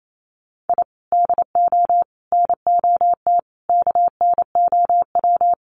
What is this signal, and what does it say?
Morse code, heard as a keyed tone, about 28 wpm, 700 Hz. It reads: I DO NOT KNOW
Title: Sets of 5 Words Encore 28wpm